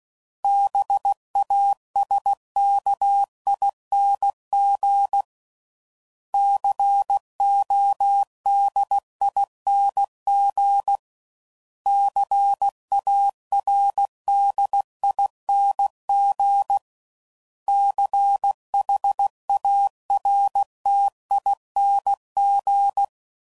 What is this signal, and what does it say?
Morse code, heard as a keyed tone, about 16 wpm, 800 Hz. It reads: BASKING CODING CARDING CHARTING
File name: Suffixes / Special INGs / Sngl Ings